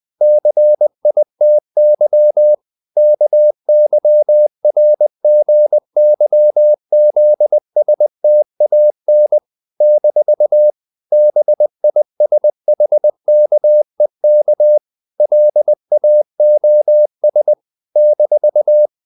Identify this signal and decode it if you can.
Morse code, heard as a keyed tone, about 20 wpm, 600 Hz. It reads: CITY KYRGYZSTAN - BISHKEK LAOS -
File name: CW / Practice / NationalCapitals20